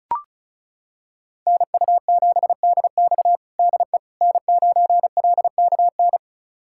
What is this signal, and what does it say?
Morse code, heard as a keyed tone, about 36 wpm, 700 Hz. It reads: NU7DX DE N9LKN
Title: USA Callsign DE Callsign - 5 Characters - 35wpm